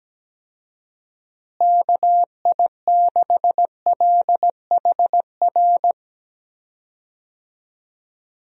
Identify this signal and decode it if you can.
Morse code, heard as a keyed tone, about 17 wpm, 700 Hz. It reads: KI6LHR
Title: USA Call Signs - 6 Characters - Repeated 3x - 17wpm